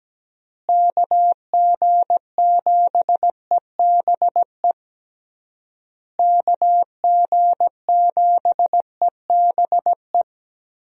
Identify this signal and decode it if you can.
Morse code, heard as a keyed tone, about 17 wpm, 700 Hz. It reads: KG7EBE KG7EBE
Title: USA Call Signs - 6 Characters - Repeated 3x - 17wpm